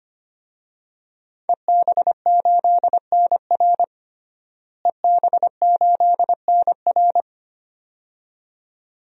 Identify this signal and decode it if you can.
Morse code, heard as a keyed tone, about 25 wpm, 700 Hz. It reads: EB8NR EB8NR
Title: DXCC-100 Call Signs - 5 Characters - Repeated 2x - 25wpm